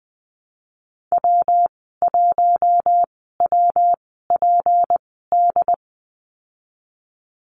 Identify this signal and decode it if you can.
Morse code, heard as a keyed tone, about 20 wpm, spaced as for 15 wpm, 700 Hz. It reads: W1WPD
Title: 161 Lesson 26g - Letter g - Call Signs 20wpm